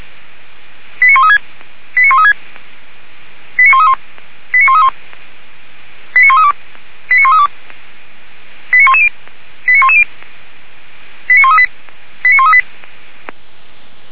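Five-tone selective call (ZVEI), heard as quick runs of five tones, tones of about 70 ms. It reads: 89136 89136 89131 89131 89132 89132 89110 89110 89138 89138
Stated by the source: ZVEI five-tone selective call received by SDR, fire brigade Orlamünde (feuerwehr-orlamuende).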